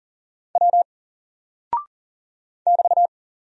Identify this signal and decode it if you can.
Morse code, heard as a keyed tone, about 40 wpm, 700 Hz. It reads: W <BT>
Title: Single Character (Letter:Number:Punctuation:Prosign) 40wpm